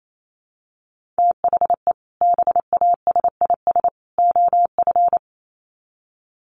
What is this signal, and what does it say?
Morse code, heard as a keyed tone, about 28 wpm, 700 Hz. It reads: THE BASIS OF
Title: Sets of 3 Words 28wpm